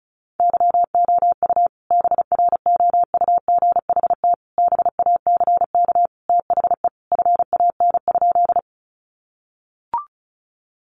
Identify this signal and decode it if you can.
Morse code, heard as a keyed tone, about 34 wpm, 700 Hz. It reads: YOU BROUGHT BACK THE FAN?
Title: Sentences from Top 1300 Words 35wpm